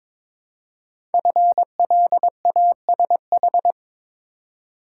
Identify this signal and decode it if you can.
Morse code, heard as a keyed tone, about 22 wpm, 700 Hz. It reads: FLASH
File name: 5 Letter Words - First 3 Letters Given - 22wpm